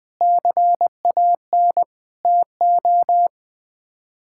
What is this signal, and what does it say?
Morse code, heard as a keyed tone, about 20 wpm, 700 Hz. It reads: CAN TO
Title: Sets of 6 Words - 20wpm